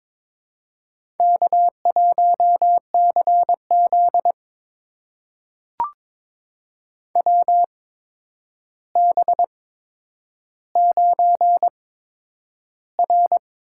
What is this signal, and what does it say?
Morse code, heard as a keyed tone, about 22 wpm, spaced as for 9 wpm, 700 Hz. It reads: K1CZ WB9R
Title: USA Call Signs - 4 Characters - 8x Character Spacing - 22wpm